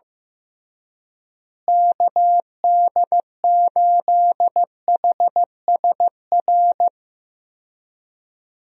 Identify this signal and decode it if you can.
Morse code, heard as a keyed tone, about 16 wpm, 700 Hz. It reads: KD8HSR
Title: USA Call Signs - 6 Characters - Repeated 2x - 15wpm